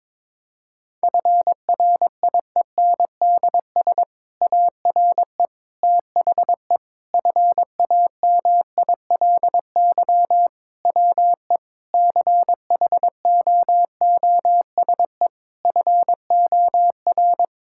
Morse code, heard as a keyed tone, about 22 wpm, 700 Hz. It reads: FRIENDS ARE THE FAMILY WE CHOOSE FOR
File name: Difficult Sentences with Hints - 22wpm